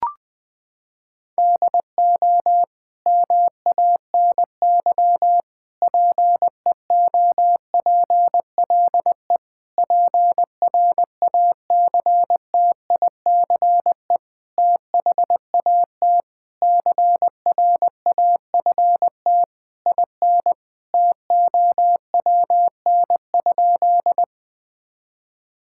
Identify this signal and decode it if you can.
Morse code, heard as a keyed tone, about 20 wpm, 700 Hz. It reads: DO MANY PEOPLE PRACTICE THAT CRAFT IN TOWN?
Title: Sentences from Top 1000 Words - Review Part 6 - 20wpm